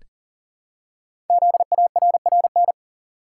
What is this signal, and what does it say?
Morse code, heard as a keyed tone, about 43 wpm, 700 Hz. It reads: ZARRN
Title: Code Groups - 5 Letters - 40wpm